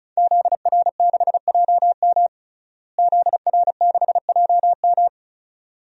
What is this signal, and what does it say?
Morse code, heard as a keyed tone, about 36 wpm, 700 Hz. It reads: ZR6JM ZR6JM
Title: DXCC-100 Call Signs - 5 Characters - Repeated 2x - 35wpm